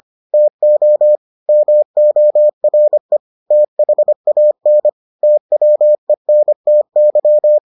Morse code, heard as a keyed tone, about 25 wpm, 600 Hz. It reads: TO MORE THAN TWENTY